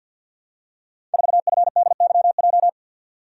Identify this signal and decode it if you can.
Morse code, heard as a keyed tone, about 45 wpm, 700 Hz. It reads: 4FDYJ
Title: Code Groups - 5 Letter:Numbers - 50wpm